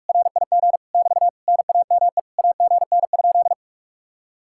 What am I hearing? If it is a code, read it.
Morse code, heard as a keyed tone, about 56 wpm, 700 Hz. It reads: RIG <BT> NAME AGN?
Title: Speed-Racing Common QSO Phrases Encore - 45wpm